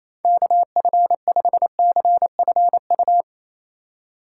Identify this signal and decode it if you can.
Morse code, heard as a keyed tone, about 27 wpm, 700 Hz. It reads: KF5CFU
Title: USA Call Signs - 6 Characters - 2x Character Spacing - 28wpm